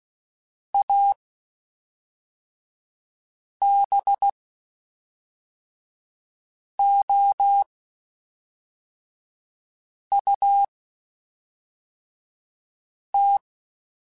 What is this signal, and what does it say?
Morse code, heard as a keyed tone, about 16 wpm, spaced as for 3 wpm, 800 Hz. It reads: ABOUT
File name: (spaced) / QSOs Spaced-mp3s / WD0FBT 1 @5 sp